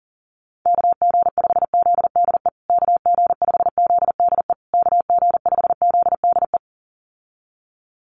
Morse code, heard as a keyed tone, about 40 wpm, 700 Hz. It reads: KG5ZDE KG5ZDE KG5ZDE
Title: USA Call Signs Repeated 3x 40wpm